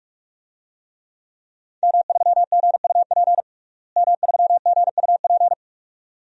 Morse code, heard as a keyed tone, about 46 wpm, 700 Hz. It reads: M3GUP M3GUP
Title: DXCC-100 Call Signs - 5 Characters - Repeated 2x - 45wpm